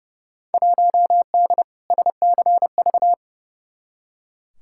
Morse code, heard as a keyed tone, about 30 wpm, 700 Hz. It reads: 1D SCV
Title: Field Day Exchanges Encore - 30wpm